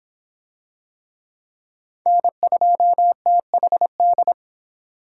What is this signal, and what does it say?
Morse code, heard as a keyed tone, about 21 wpm, 700 Hz. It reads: N2THD